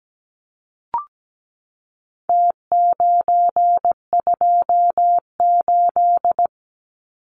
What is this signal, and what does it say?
Morse code, heard as a keyed tone, about 17 wpm, 700 Hz. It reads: T928
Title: Numbers - Contest Style - 4-Digit 17wpm